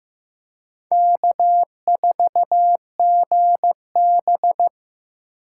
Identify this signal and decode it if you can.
Morse code, heard as a keyed tone, about 15 wpm, 700 Hz. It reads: K4GB
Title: USA Call Signs - 4 Characters - 8x Character Spacing - 15wpm